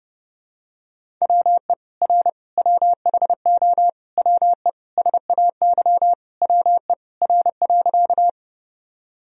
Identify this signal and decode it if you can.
Morse code, heard as a keyed tone, about 30 wpm, 700 Hz. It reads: WE R WHO WE SAY WE R.
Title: Sentences from Top 200 Words and Abbreviations 30wpm